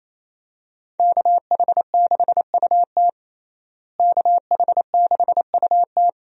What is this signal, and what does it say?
Morse code, heard as a keyed tone, about 27 wpm, 700 Hz. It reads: KH6UT KH6UT
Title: USA Call Signs - 5 Characters - Repeated 2x - 28wpm